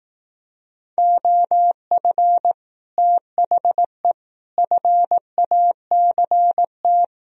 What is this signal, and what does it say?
Morse code, heard as a keyed tone, about 18 wpm, 700 Hz. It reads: OF THE FACT
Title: Speed-Racing - Sets of 4 Words 15wpm